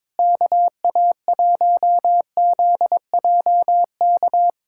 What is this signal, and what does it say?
Morse code, heard as a keyed tone, about 22 wpm, 700 Hz. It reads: KA1ZJK
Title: USA Callsign DE Callsign - 6 Characters - 22wpm